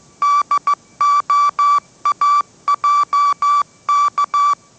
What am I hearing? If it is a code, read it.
Morse code, heard as a keyed tone, about 15 wpm, 1160 Hz. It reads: DOAJK